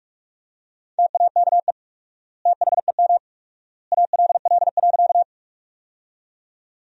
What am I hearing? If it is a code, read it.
Morse code, heard as a keyed tone, about 45 wpm, 700 Hz. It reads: TAKE THEM ALL.
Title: 4x Word-Spacing - Sentences from Top 200 Words - 45wpm